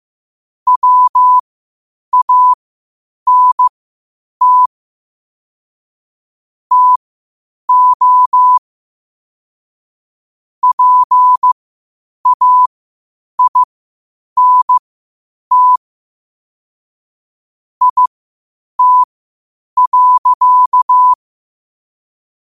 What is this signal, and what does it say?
Morse code, heard as a keyed tone, about 15 wpm, spaced as for 8 wpm, 1000 Hz. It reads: WANT TO PAINT IT.
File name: qotdfiles / 8wpm